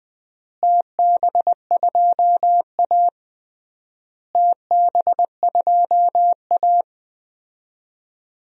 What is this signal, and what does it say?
Morse code, heard as a keyed tone, about 21 wpm, 700 Hz. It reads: TB2A TB2A
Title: DXCC-100 Call Signs - 4 Characters - Repeated 3x - 20wpm